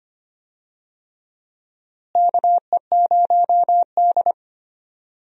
Morse code, heard as a keyed tone, about 25 wpm, 700 Hz. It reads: KE0D
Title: USA Call Signs - 4 Characters - Repeated 3x - 25wpm